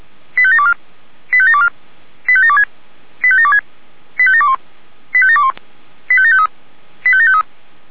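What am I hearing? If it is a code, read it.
Five-tone selective call (ZVEI), heard as quick runs of five tones, tones of about 70 ms. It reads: 85724 85724 85726 85726 85721 85721 85753 85753
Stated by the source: ZVEI five-tone call received by SDR.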